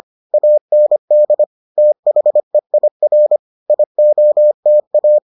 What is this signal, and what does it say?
Morse code, heard as a keyed tone, about 25 wpm, 600 Hz. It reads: AND THEIR IOTA